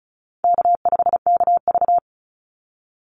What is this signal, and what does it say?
Morse code, heard as a keyed tone, about 34 wpm, 700 Hz. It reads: K5KV
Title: Call Signs Repeated 3x - 35wpm